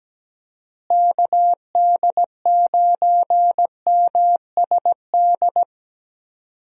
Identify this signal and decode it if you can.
Morse code, heard as a keyed tone, about 17 wpm, 700 Hz. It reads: KD9MSD
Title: USA Call Signs - 6 Characters - Repeated 2x - 17wpm